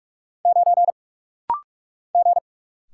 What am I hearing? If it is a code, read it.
Morse code, heard as a keyed tone, about 44 wpm, 700 Hz. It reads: 9 G
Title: Single Letter-Number Alt - Rapid-Fire 45wpm